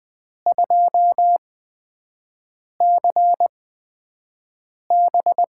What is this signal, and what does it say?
Morse code, heard as a keyed tone, about 20 wpm, spaced as for 5 wpm, 700 Hz. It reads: 2CB
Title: USA Call Signs - 4 Characters - 8x Character Spacing - 20wpm